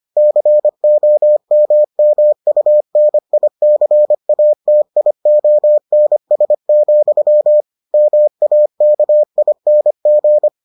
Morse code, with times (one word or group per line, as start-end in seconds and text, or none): 0.2-7.6 COMMUNICATIONS,
7.9-10.5 MAKING